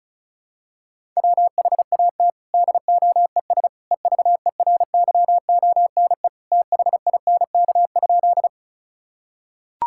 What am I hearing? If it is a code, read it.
Morse code, heard as a keyed tone, about 36 wpm, 700 Hz. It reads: WHAT DOES EVERYONE THINK?
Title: Sentences from Top 1000 Words - Review Part 5 - 35wpm